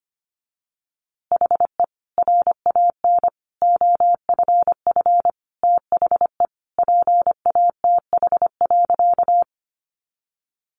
Morse code, heard as a keyed tone, about 25 wpm, 700 Hz. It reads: HE RAN OFF THE PATH.
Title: A Sound Of Thunder 25wpm